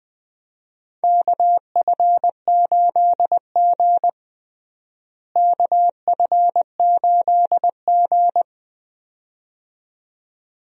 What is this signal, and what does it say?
Morse code, heard as a keyed tone, about 20 wpm, 700 Hz. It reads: KF8G KF8G
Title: USA Call Signs - 4 Characters - Repeated 2x - 20wpm